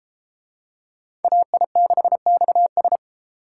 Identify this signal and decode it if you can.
Morse code, heard as a keyed tone, about 31 wpm, 700 Hz. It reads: AI6XS